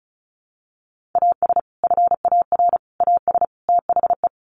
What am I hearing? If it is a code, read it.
Morse code, heard as a keyed tone, about 34 wpm, 700 Hz. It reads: AS FAR AS THE
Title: Sets of 4 Words 35wpm